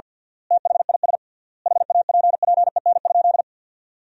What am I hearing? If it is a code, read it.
Morse code, heard as a keyed tone, about 50 wpm, 700 Hz. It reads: THIS HAPPEN?
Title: 3x Word-Spacing - Sentences from Top 400 Words - 50wpm